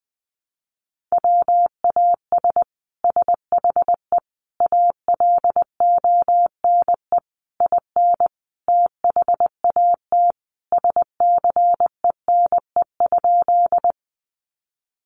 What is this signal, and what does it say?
Morse code, heard as a keyed tone, about 20 wpm, 700 Hz. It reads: WAS SHE ALONE IN THAT SCENE?
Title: Sentences from Top 1100 Words 20wpm